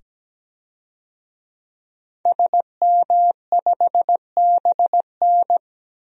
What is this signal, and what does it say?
Morse code, heard as a keyed tone, about 17 wpm, 700 Hz. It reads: SM5BN
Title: DXCC-100 Call Signs - 5 Characters - 17wpm